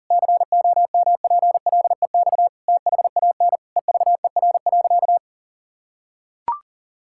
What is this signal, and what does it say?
Morse code, heard as a keyed tone, about 37 wpm, 700 Hz. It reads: COMPLEX THAN EVER.
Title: Sentences from Top 1000 Words - Review Part 6 - 40wpm